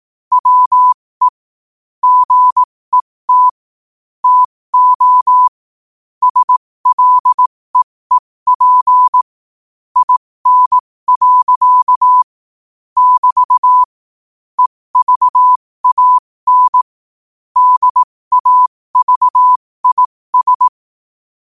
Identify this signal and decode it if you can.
Morse code, heard as a keyed tone, about 18 wpm, spaced as for 15 wpm, 1000 Hz. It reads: WE GET TO SLEEP IN. <BT> EVAN DAVIS
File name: qotdfiles / 15wpm